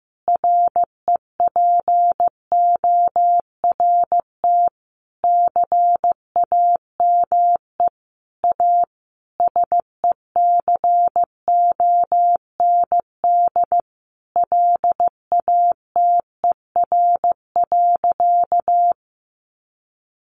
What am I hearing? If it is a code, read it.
Morse code, heard as a keyed tone, about 15 wpm, 700 Hz. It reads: REPORT CAME A SECOND LATER.
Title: Sentences from Top 900 Words 15wpm